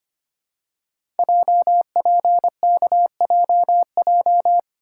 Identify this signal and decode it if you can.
Morse code, heard as a keyed tone, about 25 wpm, 700 Hz. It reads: JPKJJ
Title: Code Groups - 5 Letter:Numbers - 25wpm